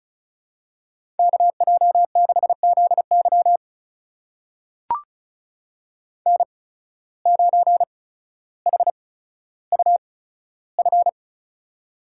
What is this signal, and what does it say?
Morse code, heard as a keyed tone, about 34 wpm, spaced as for 15 wpm, 700 Hz. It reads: KJ6ZY N9HUF
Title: USA Call Signs - 5 Characters - 8x Character Spacing - 35wpm